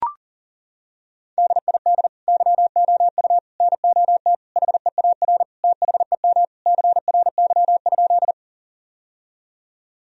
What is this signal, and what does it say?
Morse code, heard as a keyed tone, about 40 wpm, 700 Hz. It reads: DID YOU NOT HEAR THEM CRY?
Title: Sentences from Top 1000 Words - Review Part 2 - 40wpm